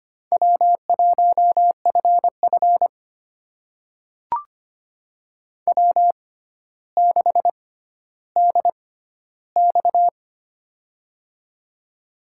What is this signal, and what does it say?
Morse code, heard as a keyed tone, about 25 wpm, spaced as for 13 wpm, 700 Hz. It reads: W1FF W6DX
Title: USA Call Signs - 4 Characters - 6x Character Spacing - 25wpm